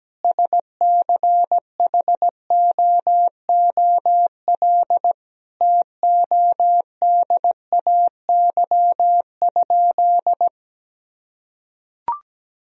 Morse code, heard as a keyed tone, about 17 wpm, 700 Hz. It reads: SCHOOL TODAY?
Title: Sentences from Top 1000 Words - Review Part 4 - 17wpm